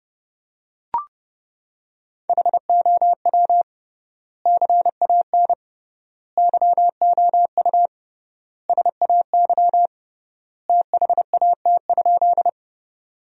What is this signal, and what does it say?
Morse code, heard as a keyed tone, about 31 wpm, 700 Hz. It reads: HOW CAN YOU SAY THAT?
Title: 3x Word-Spacing - Sentences from Top 200 Words - 30wpm